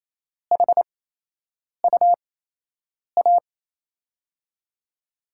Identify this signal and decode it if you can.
Morse code, heard as a keyed tone, about 28 wpm, spaced as for 8 wpm, 700 Hz. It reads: HUA
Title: USA Call Signs - 5 Characters - 8x Character Spacing - 28wpm